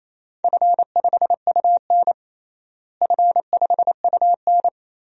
Morse code, heard as a keyed tone, about 27 wpm, 700 Hz. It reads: F5UN F5UN
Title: DXCC-100 Call Signs - 4 Characters - Repeated 2x - 28wpm